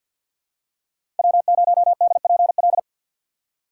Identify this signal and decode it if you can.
Morse code, heard as a keyed tone, about 47 wpm, 700 Hz. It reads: W0DPL
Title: USA Call Signs - 5 Characters - 3x Character Spacing - 50wpm